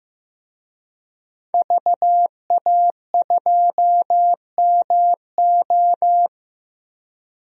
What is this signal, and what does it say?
Morse code, heard as a keyed tone, about 15 wpm, 700 Hz. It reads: VA2MO
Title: DXCC-100 Call Signs - 5 Characters - Repeated 3x - 15wpm